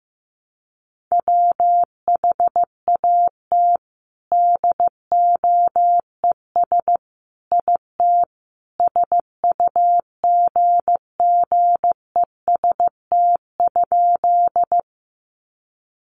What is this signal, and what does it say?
Morse code, heard as a keyed tone, about 15 wpm, 700 Hz. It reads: WHAT DOES IT SUGGEST?
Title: Sentences from Top 1300 Words 15wpm